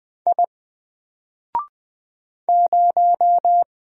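Morse code, heard as a keyed tone, about 20 wpm, 700 Hz. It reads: I 0
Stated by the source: Single letter and number Morse drill at 20 wpm, from N4LCD.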